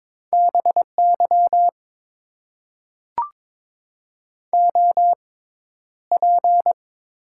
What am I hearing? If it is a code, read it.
Morse code, heard as a keyed tone, about 22 wpm, spaced as for 11 wpm, 700 Hz. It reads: BY OP
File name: Top 90 QSO Elements - 6x Character Spacing - 22wpm